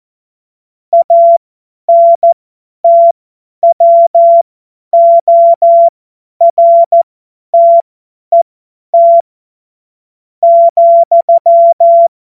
Morse code, heard as a keyed tone, about 14 wpm, spaced as for 9 wpm, 680 Hz. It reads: ANTWORTET,
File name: Morsekurs / Witze / Witz 21. September 2021, 50 F70